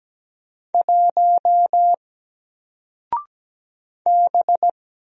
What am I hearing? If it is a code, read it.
Morse code, heard as a keyed tone, about 17 wpm, 700 Hz. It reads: 1 B
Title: Single Character (Letter:Number:Punctuation:Prosign) Alt 17wpm